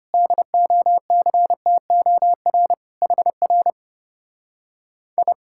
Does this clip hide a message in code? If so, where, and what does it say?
Yes: Morse code from 0.1 to 5.3 s, heard as a keyed tone, about 33 wpm, 700 Hz. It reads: DOCTOR HR I